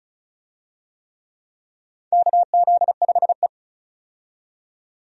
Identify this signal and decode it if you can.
Morse code, heard as a keyed tone, about 36 wpm, 700 Hz. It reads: KZ5E
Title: USA Call Signs - 4 Characters - 35wpm